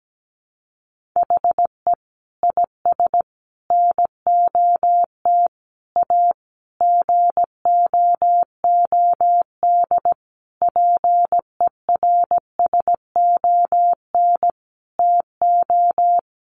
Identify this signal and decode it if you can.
Morse code, heard as a keyed tone, about 17 wpm, 700 Hz. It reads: HE IS NOT A GOOD PERSON TO
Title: Sentences from Top 1400 Words 17wpm